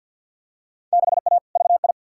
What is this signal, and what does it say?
Morse code, heard as a keyed tone, about 52 wpm, 700 Hz. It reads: 6A VI
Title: Field Day Exchanges Encore - 50wpm